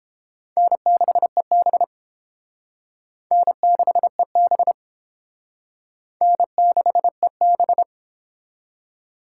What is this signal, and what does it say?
Morse code, heard as a keyed tone, about 30 wpm, 700 Hz. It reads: N6EB N6EB N6EB